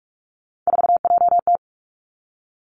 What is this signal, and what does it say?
Morse code, heard as a keyed tone, about 46 wpm, 700 Hz. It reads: 4JT
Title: Three Challenging Letter-Number Combinations 45wpm